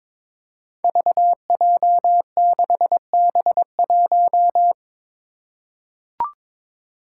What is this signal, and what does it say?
Morse code, heard as a keyed tone, about 22 wpm, 700 Hz. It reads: VJ6B1
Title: Five Challenging Letter-Number Combinations 22wpm